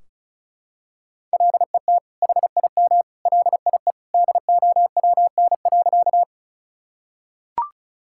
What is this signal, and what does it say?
Morse code, heard as a keyed tone, about 36 wpm, 700 Hz. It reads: LET HIM LIE DOWN.
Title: Sentences from Top 1600 Words 35wpm